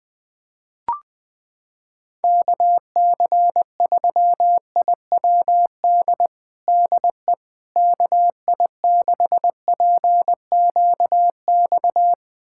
Morse code, heard as a keyed tone, about 21 wpm, 700 Hz. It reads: KC3IWD DE KI6PQX
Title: USA Callsign DE Callsign - 6 Characters - 20wpm